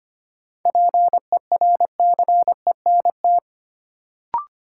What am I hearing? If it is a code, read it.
Morse code, heard as a keyed tone, about 25 wpm, 700 Hz. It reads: PERCENT
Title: Sets of 2 Words - New Word Drill - 25wpm